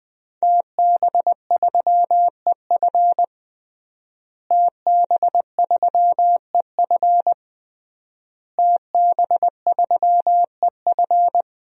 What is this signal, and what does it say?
Morse code, heard as a keyed tone, about 20 wpm, 700 Hz. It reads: TB3EF TB3EF TB3EF
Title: DXCC-100 Call Signs - 5 Characters - Repeated 3x - 20wpm